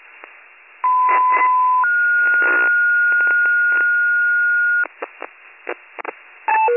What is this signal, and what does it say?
Two-tone page, heard as a one-second tone, then a three-second tone, about 1010 then 1430 Hz. It is heard from 0.8 to 4.8 s.